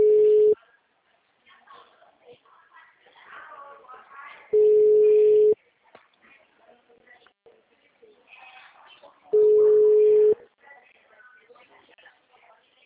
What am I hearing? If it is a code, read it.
Ringback tone (Europe, 425 Hz), with speech.
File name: out- -2013- - - .48652